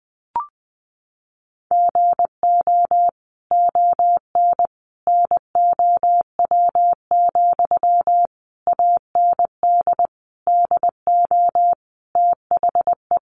Morse code, heard as a keyed tone, about 20 wpm, 700 Hz. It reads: GO ON NOW, AND DO THE
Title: Sentences from Top 1500 Words 20wpm